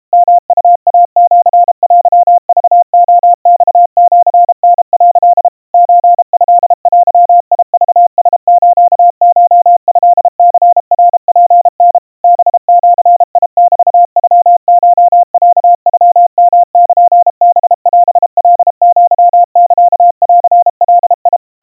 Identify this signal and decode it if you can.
Morse code (Japanese Wabun code), heard as a keyed tone, about 32 wpm, 700 Hz. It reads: ヨウイシテクレマシタ」 ソトデクラスコトニナツタ ハジメノコロノヨルハオカアサンガ
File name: kawa / Morse sample / kamyu 120 cpm 700 Hz Crispy